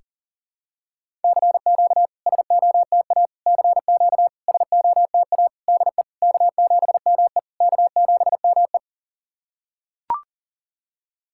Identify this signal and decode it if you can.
Morse code, heard as a keyed tone, about 40 wpm, 690 Hz. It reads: CQ SOTA CQ SOTA DE K7ME K7ME
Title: SOTA QSO Practice 40wpm